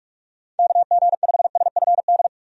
Morse code, heard as a keyed tone, about 44 wpm, 700 Hz. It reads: KG5SFD
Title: USA Call Signs - 6 Characters - 45wpm